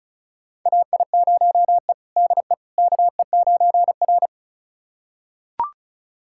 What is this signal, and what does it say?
Morse code, heard as a keyed tone, about 34 wpm, 700 Hz. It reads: AI0E DE KE9R
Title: USA Callsign DE Callsign - 4 Characters - 35wpm